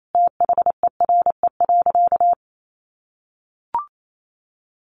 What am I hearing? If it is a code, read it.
Morse code, heard as a keyed tone, about 28 wpm, 700 Hz. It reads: THERE.
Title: Sentences from Top 1300 Words 28wpm